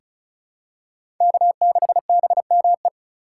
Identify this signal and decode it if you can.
Morse code, heard as a keyed tone, about 34 wpm, 700 Hz. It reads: K6BME
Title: USA Call Signs - 5 Characters - 2x Character Spacing - 35wpm